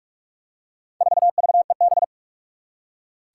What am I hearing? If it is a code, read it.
Morse code, heard as a keyed tone, about 43 wpm, 700 Hz. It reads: 4VEB
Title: Four Challenging Letter-Number Combinations 45wpm